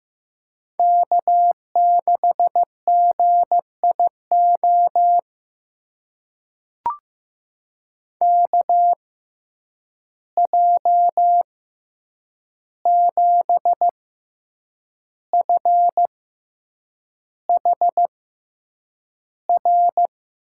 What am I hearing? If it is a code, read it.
Morse code, heard as a keyed tone, about 15 wpm, spaced as for 7 wpm, 700 Hz. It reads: K6GIO KJ7FHR
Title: USA Call Signs - 6x Character Spacing - 15wpm